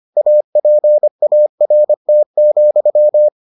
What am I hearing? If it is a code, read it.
Morse code, heard as a keyed tone, about 25 wpm, 600 Hz. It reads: APART,